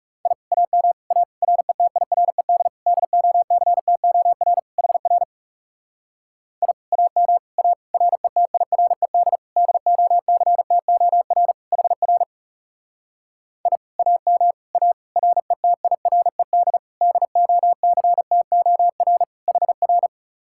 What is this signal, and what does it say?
Morse code, heard as a keyed tone, about 36 wpm, 700 Hz. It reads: I AM A RETIRED DOCTOR HR I AM A RETIRED DOCTOR HR I AM A RETIRED DOCTOR HR
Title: Speed-Racing Common QSO Phrases Encore - 30wpm